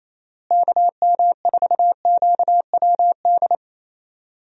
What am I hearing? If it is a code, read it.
Morse code, heard as a keyed tone, about 28 wpm, 700 Hz. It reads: KM4QWD
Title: USA Call Signs - 6 Characters - Repeated 3x - 28wpm